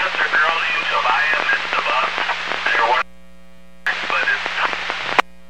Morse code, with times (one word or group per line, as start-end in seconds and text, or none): none